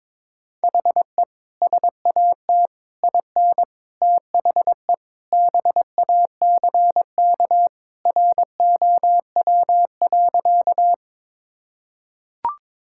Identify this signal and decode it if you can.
Morse code, heard as a keyed tone, about 22 wpm, 700 Hz. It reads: HE SAT IN THE BACK ROW.
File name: practice / 22wpm / Sentences with 3-Word Phrases - 22wpm